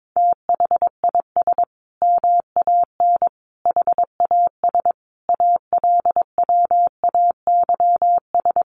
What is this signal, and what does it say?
Morse code, heard as a keyed tone, about 22 wpm, 700 Hz. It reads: THIS MAN HAS ALWAYS
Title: Sentences from Top 1300 Words 22wpm